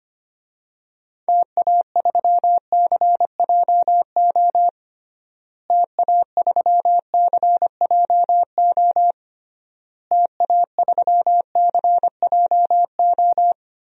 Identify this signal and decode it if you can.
Morse code, heard as a keyed tone, about 25 wpm, 700 Hz. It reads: TA3CJO TA3CJO TA3CJO
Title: DXCC-100 Call Signs - 6 Characters - Repeated 3x - 25wpm